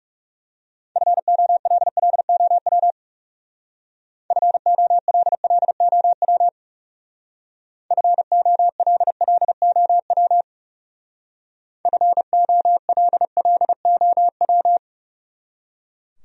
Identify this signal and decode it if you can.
Morse code, heard as a keyed tone, about 38 wpm, 700 Hz. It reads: FOLLOW FOLLOW FOLLOW FOLLOW